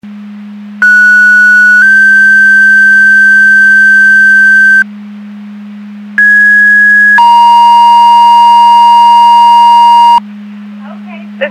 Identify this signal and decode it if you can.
Two-tone page, heard as a one-second tone, then a three-second tone, about 1470 then 1550 Hz, about 1650 then 940 Hz, with speech. It is heard from 0.8 to 10.2 s.